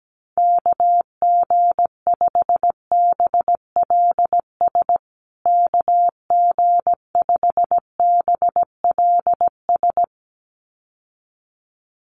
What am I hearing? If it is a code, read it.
Morse code, heard as a keyed tone, about 17 wpm, 700 Hz. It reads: KG5BLS KG5BLS
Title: USA Call Signs Repeated 2x 17wpm